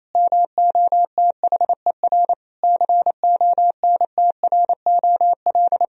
Morse code, heard as a keyed tone, about 27 wpm, 690 Hz. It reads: MOTHER CONTROL
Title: Sentences from Top 1000 Words - Review Part 3 - 28wpm